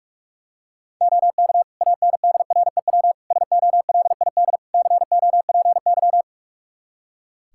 Morse code, heard as a keyed tone, about 44 wpm, 700 Hz. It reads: OK ANDREW SOLID COPY
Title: Practice QSO - Exchange 3 - More Info - 45wpm